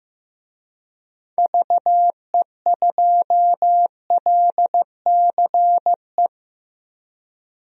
Morse code, heard as a keyed tone, about 15 wpm, 700 Hz. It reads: VE2LCE
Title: DXCC-100 Call Signs - 6 Characters - 15wpm